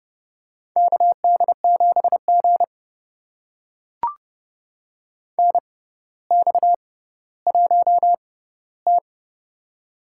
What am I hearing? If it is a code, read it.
Morse code, heard as a keyed tone, about 31 wpm, spaced as for 15 wpm, 700 Hz. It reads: KD7G NX1T
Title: USA Call Signs - 4 Characters - 6x Character Spacing - 30wpm